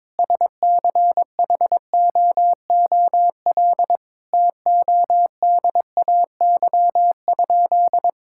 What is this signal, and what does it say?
Morse code, heard as a keyed tone, about 22 wpm, 700 Hz. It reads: SCHOOL TODAY?
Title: Sentences from Top 1000 Words - Review Part 5 - 22wpm